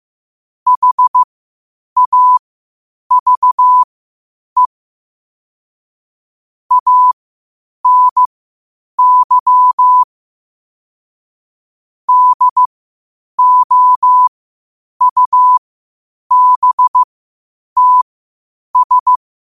Morse code, heard as a keyed tone, about 15 wpm, spaced as for 8 wpm, 1000 Hz. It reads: HAVE ANY DOUBTS
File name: qotdfiles / 8wpm